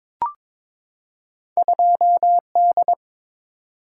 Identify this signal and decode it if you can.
Morse code, heard as a keyed tone, about 23 wpm, 700 Hz. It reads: 2D
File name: practice / 22wpm / Two Letter-Number Combinations - 22wpm